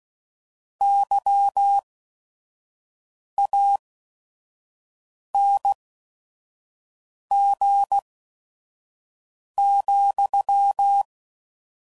Morse code, heard as a keyed tone, about 16 wpm, spaced as for 5 wpm, 800 Hz. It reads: YANG,